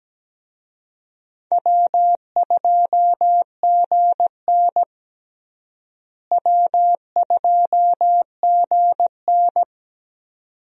Morse code, heard as a keyed tone, about 17 wpm, 700 Hz. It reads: W2GN W2GN
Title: USA Call Signs - 4 Characters - Repeated 2x - 17wpm